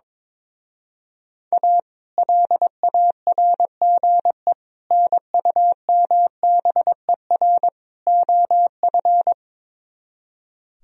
Morse code, heard as a keyed tone, about 22 wpm, 700 Hz. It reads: A LARGE NUMBER OF
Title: Speed-Racing - Sets of 4 Words 22wpm